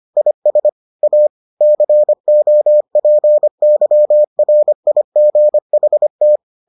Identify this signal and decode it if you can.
Morse code, heard as a keyed tone, about 25 wpm, 600 Hz. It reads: IS A COPYRIGHT